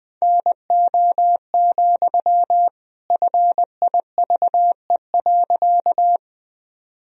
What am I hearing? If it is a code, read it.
Morse code, heard as a keyed tone, about 20 wpm, 700 Hz. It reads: NO, FIVE.
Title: Sentences from Top 1000 Words - Review Part 2 - 20wpm